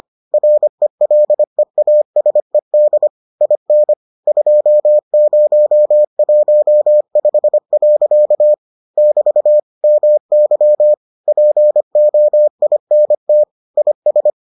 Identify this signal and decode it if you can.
Morse code, heard as a keyed tone, about 25 wpm, 600 Hz. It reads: RELEASED IN 2015. <BT> MY POINT IS